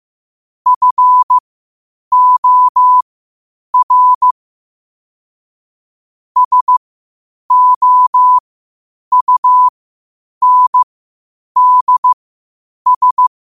Morse code, heard as a keyed tone, about 15 wpm, spaced as for 8 wpm, 1000 Hz. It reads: FOR SOUNDS
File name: qotdfiles / 8wpm